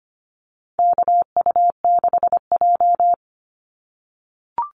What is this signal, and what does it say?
Morse code, heard as a keyed tone, about 25 wpm, 700 Hz. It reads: KU6J
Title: NA SOTA Activator Callsigns 25wpm